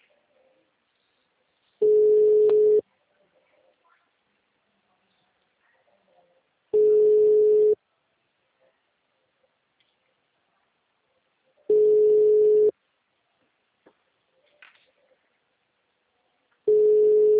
Ringback tone (Europe, 425 Hz).
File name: out- -2016- - - .22